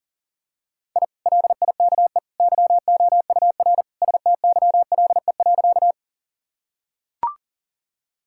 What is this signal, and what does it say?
Morse code, heard as a keyed tone, about 40 wpm, 700 Hz. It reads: I LIKE YOUR STYLE.
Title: Sentences from Top 1000 Words - Review Part 5 - 40wpm